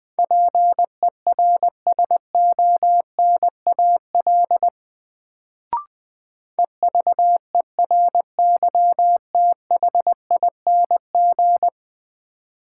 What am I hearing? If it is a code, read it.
Morse code, heard as a keyed tone, about 20 wpm, 700 Hz. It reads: PERSONAL EVERYTHING
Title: Sentences from Top 700 Words - New Word Drill - 20wpm